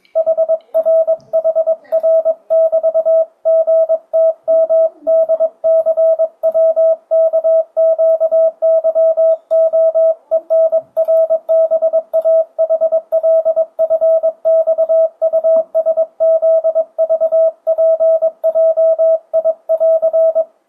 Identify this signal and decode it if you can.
Morse code, heard as a keyed tone, about 22 wpm, 650 Hz. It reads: HRHR<BT>GTMDCWKQYOENRBAHLFXUSZVPJI<AR>
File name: 2025cw / speedup / 22wpm